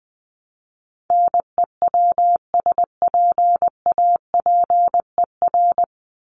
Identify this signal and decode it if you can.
Morse code, heard as a keyed tone, about 20 wpm, 700 Hz. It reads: NEWSPAPER
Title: Top 751-1000 Words 20wpm